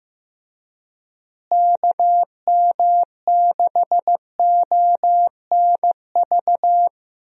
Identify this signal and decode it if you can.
Morse code, heard as a keyed tone, about 15 wpm, 700 Hz. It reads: KM6ONV
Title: USA Call Signs - 6 Characters - Repeated 3x - 15wpm